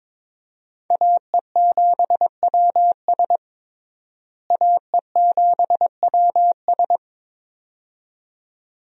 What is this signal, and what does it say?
Morse code, heard as a keyed tone, about 22 wpm, 700 Hz. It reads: AE7WS AE7WS
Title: USA Call Signs - 5 Characters - Repeated 2x - 22wpm